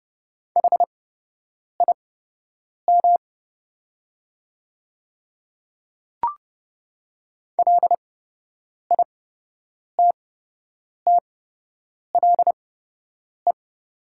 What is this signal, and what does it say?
Morse code, heard as a keyed tone, about 30 wpm, spaced as for 8 wpm, 700 Hz. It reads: HIM LITTLE
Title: 8x Character Spacing - Top 200 Words - 30wpm